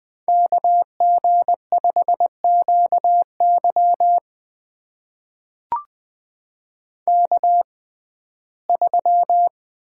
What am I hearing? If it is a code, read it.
Morse code, heard as a keyed tone, about 20 wpm, spaced as for 15 wpm, 700 Hz. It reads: KG5QY K3
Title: USA Call Signs - 5 Characters - 6x Character Spacing - 20wpm